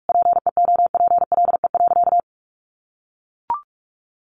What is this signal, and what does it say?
Morse code, heard as a keyed tone, about 44 wpm, 700 Hz. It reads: PEOPLE.
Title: Sentences from Top 1100 Words 45wpm